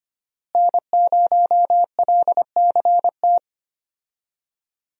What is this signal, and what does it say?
Morse code, heard as a keyed tone, about 25 wpm, 700 Hz. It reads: N0LCT
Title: USA Call Signs - 5 Characters - Repeated 2x - 25wpm